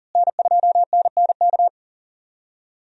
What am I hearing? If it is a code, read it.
Morse code, heard as a keyed tone, about 40 wpm, 700 Hz. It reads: N2NNK
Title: USA Call Signs - 5 Characters - Repeated 2x - 40wpm